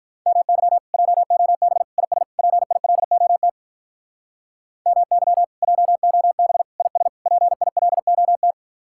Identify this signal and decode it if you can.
Morse code, heard as a keyed tone, about 44 wpm, 700 Hz. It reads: MY JOB IS PILOT MY JOB IS PILOT
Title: Speed-Racing Common QSO Phrases - 35wpm